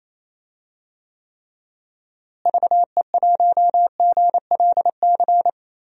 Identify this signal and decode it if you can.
Morse code, heard as a keyed tone, about 28 wpm, 700 Hz. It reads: VE1GLC
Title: DXCC-100 Call Signs - 6 Characters - Repeated 2x - 28wpm